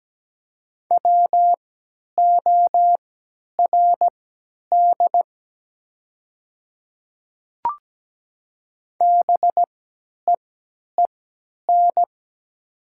Morse code, heard as a keyed tone, about 17 wpm, spaced as for 10 wpm, 700 Hz. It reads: WORD BEEN
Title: 3x Character Spacing - Top 200 Words - 17wpm